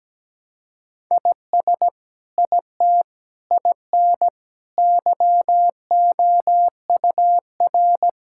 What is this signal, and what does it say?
Morse code, heard as a keyed tone, about 17 wpm, 700 Hz. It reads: IS IT IN YOUR
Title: Sentences from Top 1000 Words - Review Part 5 - 17wpm